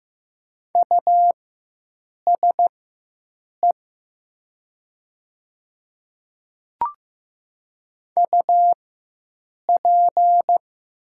Morse code, heard as a keyed tone, about 15 wpm, spaced as for 7 wpm, 700 Hz. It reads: USE UP
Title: 4x Character Spacing - Top 100 Words - 15wpm